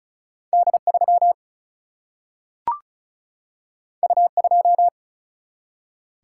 Morse code, heard as a keyed tone, about 36 wpm, 700 Hz. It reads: D3 U2
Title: Single-Letter Single-Number 35wpm